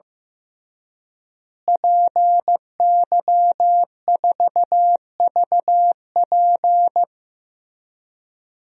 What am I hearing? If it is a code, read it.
Morse code, heard as a keyed tone, about 15 wpm, 700 Hz. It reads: PY4VP
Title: DXCC-100 Call Signs - 5 Characters - Repeated 3x - 15wpm